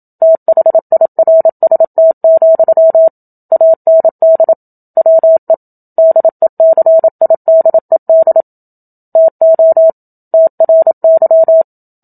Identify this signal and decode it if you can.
Morse code, heard as a keyed tone, about 27 wpm, 650 Hz. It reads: THIRST, AND WE DECIDED TO TRY
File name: Book Two / 01-Chapter 1 - 27 WPM